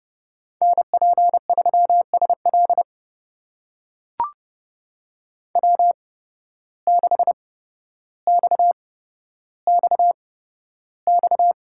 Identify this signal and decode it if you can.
Morse code, heard as a keyed tone, about 30 wpm, spaced as for 13 wpm, 700 Hz. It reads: NP3SL W6XXX
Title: USA Call Signs - 5 Characters - 8x Character Spacing - 30wpm